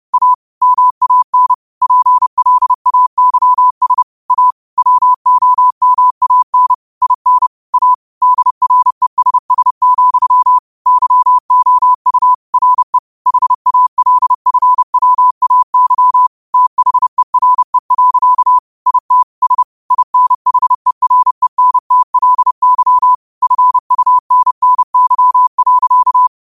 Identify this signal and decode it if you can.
Morse code, heard as a keyed tone, about 30 wpm, 1000 Hz. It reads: A MAN PLAYS A WOMAN IN A DRESS, YOU RE HALFWAY THERE. IT S INHERENTLY FUNNY.
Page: New quotes every day in morse code at 30 Words per minute.